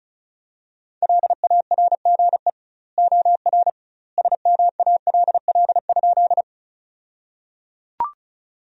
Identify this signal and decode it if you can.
Morse code, heard as a keyed tone, about 36 wpm, 700 Hz. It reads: LARGE OR SMALL?
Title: 2x Word-Spacing - Sentences from Top 200 Words - 35wpm